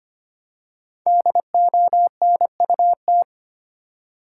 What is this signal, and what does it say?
Morse code, heard as a keyed tone, about 25 wpm, 700 Hz. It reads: DONUT
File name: practice / 25wpm / Difficult Words with Hints - 25wpm